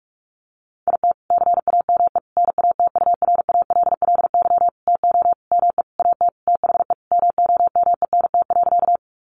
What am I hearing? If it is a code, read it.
Morse code, heard as a keyed tone, about 44 wpm, 700 Hz. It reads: IT CAME NATURALLY TO ME AT THE MOMENT.
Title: Sentences from Top 1300 Words 45wpm